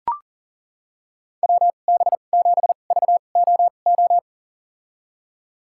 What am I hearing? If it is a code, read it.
Morse code, heard as a keyed tone, about 40 wpm, spaced as for 29 wpm, 700 Hz. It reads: WB7VOO
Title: USA Call Signs - 6 Characters - 2x Character Spacing - 40wpm